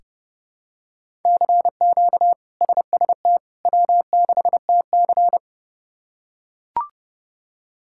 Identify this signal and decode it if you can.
Morse code, heard as a keyed tone, about 30 wpm, 700 Hz. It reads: CQ SST W6TC
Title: K1USN SST QSO Practice - 30wpm